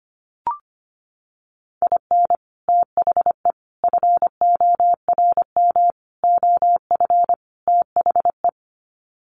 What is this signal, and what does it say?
Morse code, heard as a keyed tone, about 25 wpm, 700 Hz. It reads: IN THE FORM OF THE
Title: Sets of 5 Words Encore2 25wpm